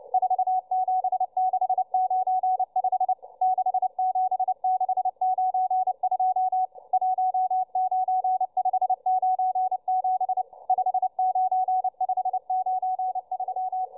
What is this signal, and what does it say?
Morse code, heard as a keyed tone, about 29 wpm, 740 Hz, with radio noise. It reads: 47695 67692 19597 59593